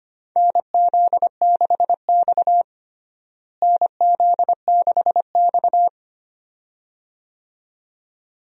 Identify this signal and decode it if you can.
Morse code, heard as a keyed tone, about 25 wpm, 700 Hz. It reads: NZ6X NZ6X
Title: USA Call Signs - 4 Characters - Repeated 2x - 25wpm